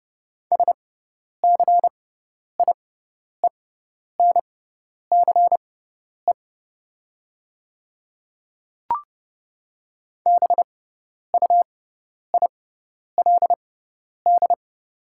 Morse code, heard as a keyed tone, about 30 wpm, spaced as for 10 wpm, 700 Hz. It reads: SCIENCE BUILD
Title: 6x Character Spacing - Top 500 Words - 30wpm